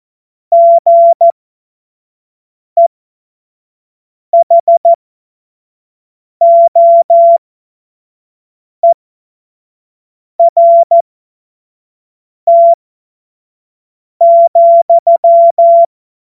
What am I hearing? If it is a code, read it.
Morse code, heard as a keyed tone, about 14 wpm, spaced as for 5 wpm, 680 Hz. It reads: GEHOERT,
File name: Morsekurs / Witze / Witz 11. April 2023 25 F70